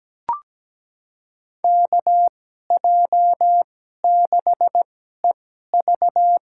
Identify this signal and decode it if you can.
Morse code, heard as a keyed tone, about 17 wpm, spaced as for 12 wpm, 700 Hz. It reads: KJ6EV
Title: USA Call Signs - 2x Character Spacing - 17wpm